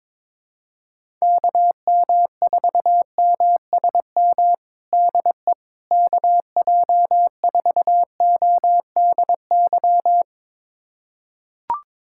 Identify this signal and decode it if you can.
Morse code, heard as a keyed tone, about 21 wpm, 700 Hz. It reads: KM4MSM DE KJ4ODY
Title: USA Callsign DE Callsign 22wpm